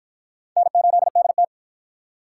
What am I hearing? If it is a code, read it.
Morse code, heard as a keyed tone, about 53 wpm, 700 Hz. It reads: N8DT